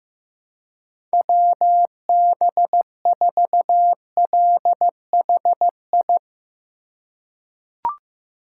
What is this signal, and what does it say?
Morse code, heard as a keyed tone, about 15 wpm, 700 Hz. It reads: WB4LHI
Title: USA Call Signs - 6 Characters - 3x Character Spacing - 15wpm